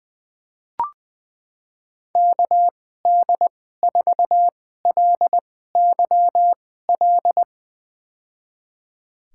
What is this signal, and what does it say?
Morse code, heard as a keyed tone, about 20 wpm, spaced as for 14 wpm, 700 Hz. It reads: KD4LYL
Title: USA Call Signs - 6 Characters - 2x Character Spacing - 20wpm